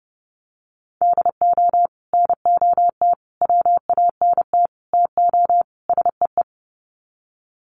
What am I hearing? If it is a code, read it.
Morse code, heard as a keyed tone, about 30 wpm, 700 Hz. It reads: DO NOT WANT TO SEE
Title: Sets of 5 Words 30wpm